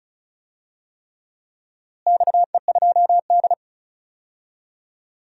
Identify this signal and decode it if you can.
Morse code, heard as a keyed tone, about 34 wpm, 700 Hz. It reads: XE2D
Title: DXCC-100 Call Signs - 4 Characters - Repeated 3x - 35wpm